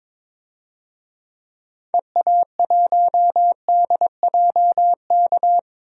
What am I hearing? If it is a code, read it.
Morse code, heard as a keyed tone, about 21 wpm, 700 Hz. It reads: EA1DJK
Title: DXCC-100 Call Signs - 6 Characters - 22wpm